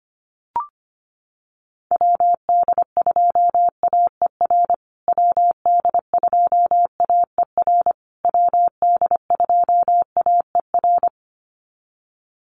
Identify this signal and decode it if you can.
Morse code, heard as a keyed tone, about 25 wpm, 700 Hz. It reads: WD2AER WD2AER WD2AER
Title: USA Call Signs Repeated 3x 25wpm